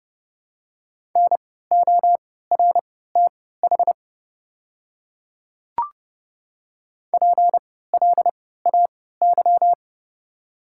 Morse code, heard as a keyed tone, about 30 wpm, spaced as for 17 wpm, 700 Hz. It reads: NORTH PLAY
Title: 3x Character Spacing - Top 300 Words - 30wpm